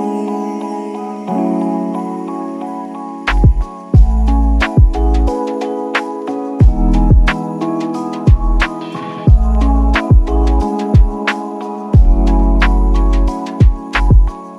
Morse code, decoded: none